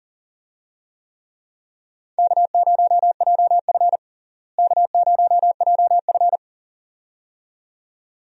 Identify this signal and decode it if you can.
Morse code, heard as a keyed tone, about 40 wpm, 700 Hz. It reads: K0JF K0JF
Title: USA Call Signs - 4 Characters - Repeated 2x - 40wpm